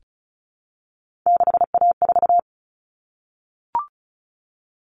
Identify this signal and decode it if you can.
Morse code, heard as a keyed tone, about 34 wpm, 700 Hz. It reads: 6A4
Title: Common Cut Numbers - 3-Digit 35wpm